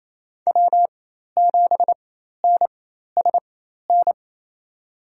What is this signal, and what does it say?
Morse code, heard as a keyed tone, about 28 wpm, spaced as for 13 wpm, 700 Hz. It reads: W7NSN
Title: USA Call Signs - 4x Character Spacing - 28wpm